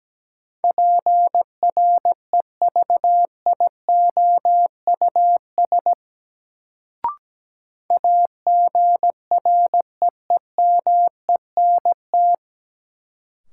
Morse code, heard as a keyed tone, about 17 wpm, 700 Hz. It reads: PREVIOUS AGREEMENT
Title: Sentences from Top 1500 Words - New Word Drill Alt - 17wpm